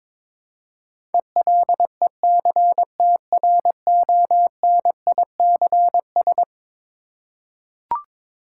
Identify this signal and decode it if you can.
Morse code, heard as a keyed tone, about 22 wpm, 700 Hz. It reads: ELECTRONICS
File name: practice / 22wpm / Eleven Letter Words - 22wpm